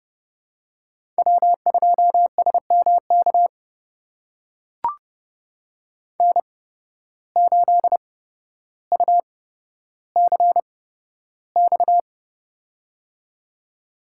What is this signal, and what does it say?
Morse code, heard as a keyed tone, about 30 wpm, spaced as for 13 wpm, 700 Hz. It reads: W2SMK N8UCX
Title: USA Call Signs - 5 Characters - 8x Character Spacing - 30wpm